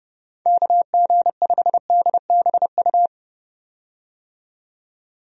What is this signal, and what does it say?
Morse code, heard as a keyed tone, about 30 wpm, 700 Hz. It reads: KG5DBU
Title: USA Call Signs - 6 Characters - 30wpm